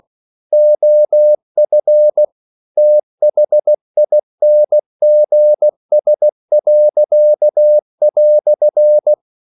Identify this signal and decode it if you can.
Morse code, heard as a keyed tone, about 16 wpm, 600 Hz. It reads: OF THINGS."